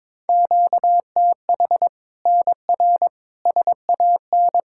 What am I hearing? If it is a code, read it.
Morse code, heard as a keyed tone, about 22 wpm, 700 Hz. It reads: QTH NR SAN
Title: Practice QSO - Exchange 1 - Beginning - 22wpm